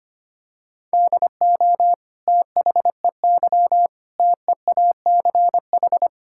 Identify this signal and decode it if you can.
Morse code, heard as a keyed tone, about 25 wpm, 700 Hz. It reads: DO THEY TEACH
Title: Sentences from Top 1000 Words - Review Part 6 - 25wpm